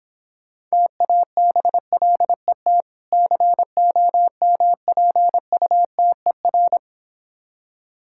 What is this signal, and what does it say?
Morse code, heard as a keyed tone, about 26 wpm, 700 Hz. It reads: TABLET COMPUTER